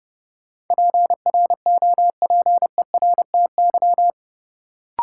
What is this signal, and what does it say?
Morse code, heard as a keyed tone, about 31 wpm, 700 Hz. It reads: PROPERTY
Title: Sentences from Top 1000 Words - New Word Drill - 30wpm